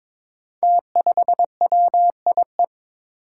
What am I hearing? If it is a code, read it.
Morse code, heard as a keyed tone, about 23 wpm, 700 Hz. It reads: T5WIE
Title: Five Challenging Letter-Number Combinations 22wpm